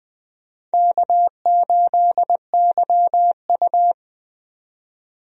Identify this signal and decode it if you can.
Morse code, heard as a keyed tone, about 20 wpm, 700 Hz. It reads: K8YU
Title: USA Call Signs - 4 Characters - 20wpm